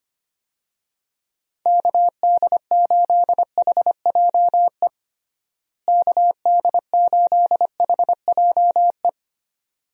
Morse code, heard as a keyed tone, about 25 wpm, 700 Hz. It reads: KD8HJE KD8HJE
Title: USA Call Signs - 6 Characters - Repeated 2x - 25wpm